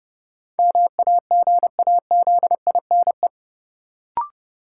Morse code, heard as a keyed tone, about 30 wpm, 700 Hz. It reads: MAGAZINE
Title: Sentences from Top 1000 Words - Review Part 7 - New Word Drill - 30wpm